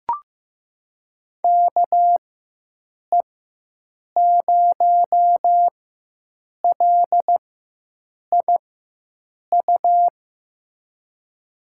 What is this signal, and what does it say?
Morse code, heard as a keyed tone, about 15 wpm, spaced as for 7 wpm, 700 Hz. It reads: KE0LIU
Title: USA Call Signs - 6 Characters - 4x Character Spacing - 15wpm